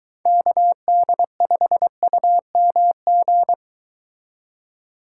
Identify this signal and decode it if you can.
Morse code, heard as a keyed tone, about 23 wpm, 700 Hz. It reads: KD5UMG